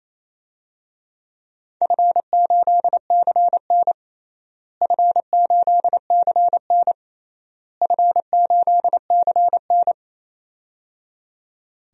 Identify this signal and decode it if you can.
Morse code, heard as a keyed tone, about 27 wpm, 700 Hz. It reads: F8CN F8CN F8CN
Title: DXCC-100 Call Signs - 4 Characters - Repeated 3x - 28wpm